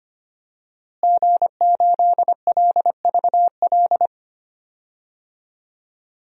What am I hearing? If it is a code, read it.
Morse code, heard as a keyed tone, about 25 wpm, 700 Hz. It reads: G8LVL
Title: DXCC-100 Call Signs - 5 Characters - Repeated 3x - 25wpm